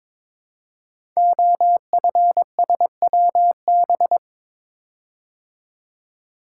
Morse code, heard as a keyed tone, about 22 wpm, 700 Hz. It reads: OFSWB
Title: Code Groups - 5 Letters - 22wpm